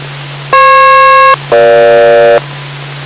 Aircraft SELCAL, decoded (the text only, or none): FP-EG